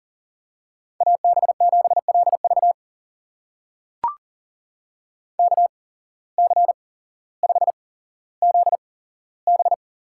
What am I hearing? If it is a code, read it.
Morse code, heard as a keyed tone, about 40 wpm, spaced as for 17 wpm, 700 Hz. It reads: AB7LV KC5ZB
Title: USA Call Signs - 5 Characters - 8x Character Spacing - 40wpm